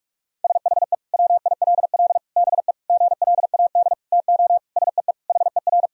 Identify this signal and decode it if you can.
Morse code, heard as a keyed tone, about 46 wpm, 700 Hz. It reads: SHE WILL BE GLAD TO SEE HER
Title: Sentences from Top 1000 Words - Review Part 2 - 45wpm